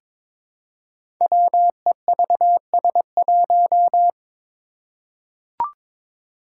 Morse code, heard as a keyed tone, about 22 wpm, 700 Hz. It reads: WEVS1
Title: Five Challenging Letter-Number Combinations 22wpm